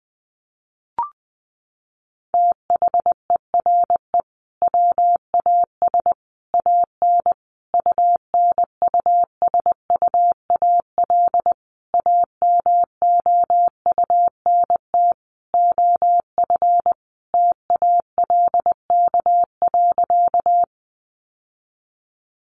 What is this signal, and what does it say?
Morse code, heard as a keyed tone, about 20 wpm, 700 Hz. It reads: THERE WAS AN UNUSUAL AMOUNT OF TALK.
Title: Sentences from Top 1400 Words 20wpm